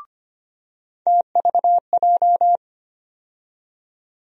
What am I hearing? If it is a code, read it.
Morse code, heard as a keyed tone, about 25 wpm, 700 Hz. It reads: TVJ
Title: USA Call Sign Postfixes - 3 Letters 25wpm